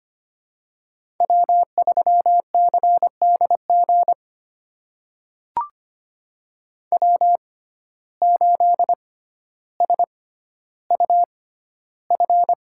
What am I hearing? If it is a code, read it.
Morse code, heard as a keyed tone, about 25 wpm, spaced as for 13 wpm, 700 Hz. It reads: W3CDG W8SUF
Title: USA Call Signs - 5 Characters - 6x Character Spacing - 25wpm